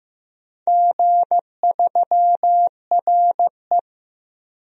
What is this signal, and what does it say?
Morse code, heard as a keyed tone, about 15 wpm, 700 Hz. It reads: G3RE
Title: DXCC-100 Call Signs - 4 Characters - Repeated 3x - 15wpm